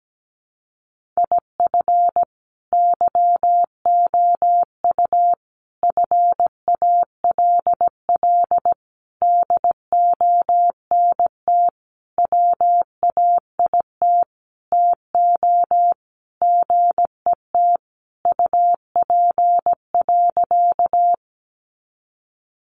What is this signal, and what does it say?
Morse code, heard as a keyed tone, about 17 wpm, 700 Hz. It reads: IF YOU FALL DONT WAIT TO GET UP.
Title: Sentences from Top 500 Words 17wpm